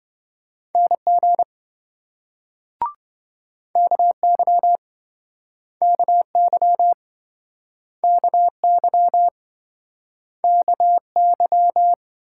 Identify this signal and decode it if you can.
Morse code, heard as a keyed tone, about 28 wpm, 700 Hz. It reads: NG KY KY KY KY